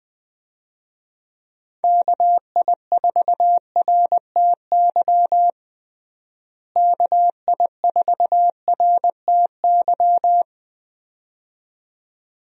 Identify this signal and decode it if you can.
Morse code, heard as a keyed tone, about 20 wpm, 700 Hz. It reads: KI4RTY KI4RTY
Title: USA Call Signs - 6 Characters - Repeated 2x - 20wpm